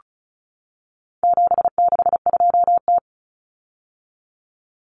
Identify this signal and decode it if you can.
Morse code, heard as a keyed tone, about 34 wpm, 700 Hz. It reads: 762T
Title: Common Cut Numbers - 4-Digit 35wpm